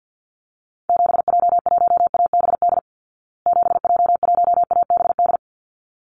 Morse code, heard as a keyed tone, about 50 wpm, 700 Hz. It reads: 7J1ABD 7J1ABD
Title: Call Signs Repeated 3x - 50wpm